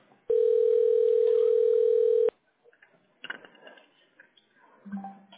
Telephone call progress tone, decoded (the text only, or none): ringback tone